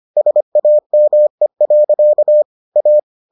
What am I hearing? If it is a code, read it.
Morse code, heard as a keyed tone, about 25 wpm, 600 Hz. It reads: SAME. A